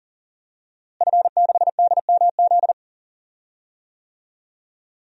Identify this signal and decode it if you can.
Morse code, heard as a keyed tone, about 40 wpm, 700 Hz. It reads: F6DMZ
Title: DXCC-100 Call Signs - 5 Characters - Repeated 2x - 40wpm